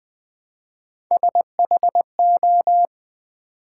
Morse code, heard as a keyed tone, about 20 wpm, 700 Hz. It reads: SHO
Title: Build A Word - 1 Letter At A Time - 3 to 5-Letter Words - 20wpm